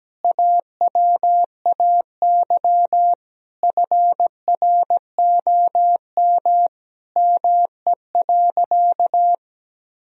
Morse code, heard as a keyed tone, about 17 wpm, 700 Hz. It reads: AWAY FROM ME.
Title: Sentences from Top 1000 Words - Review Part 2 - 17wpm